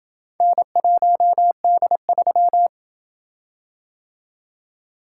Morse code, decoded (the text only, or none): N1D3